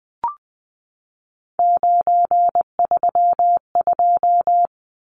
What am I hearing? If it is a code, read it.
Morse code, heard as a keyed tone, about 20 wpm, 700 Hz. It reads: 932
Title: Numbers - 3-Digit 20wpm